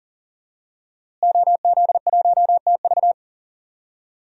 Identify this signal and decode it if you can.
Morse code, heard as a keyed tone, about 40 wpm, 700 Hz. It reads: OZ1TV
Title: DXCC-100 Call Signs - 5 Characters - Repeated 3x - 40wpm